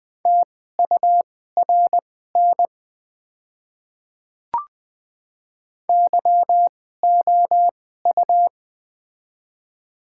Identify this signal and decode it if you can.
Morse code, heard as a keyed tone, about 20 wpm, spaced as for 14 wpm, 700 Hz. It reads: TURN YOU
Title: 2x Character Spacing - Top 300 Words - 20wpm